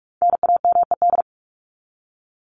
Morse code, heard as a keyed tone, about 44 wpm, 700 Hz. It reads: NAMED
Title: ED Suffix - Top 100 Words with Base Words 45wpm